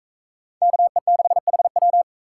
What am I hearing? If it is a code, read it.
Morse code, heard as a keyed tone, about 42 wpm, 700 Hz. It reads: KE6HW